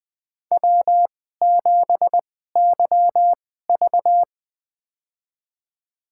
Morse code, heard as a keyed tone, about 20 wpm, spaced as for 14 wpm, 700 Hz. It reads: W7YV
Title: USA Call Signs - 4 Characters - 2x Character Spacing - 20wpm